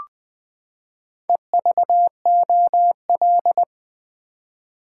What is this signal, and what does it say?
Morse code, heard as a keyed tone, about 20 wpm, 700 Hz. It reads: EVOL
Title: Build A Word - 4 Letters At A Time - 12-Letter Words - 20wpm